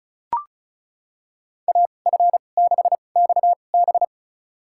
Morse code, heard as a keyed tone, about 34 wpm, spaced as for 25 wpm, 700 Hz. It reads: AF6XB
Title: USA Call Signs - 5 Characters - 2x Character Spacing - 35wpm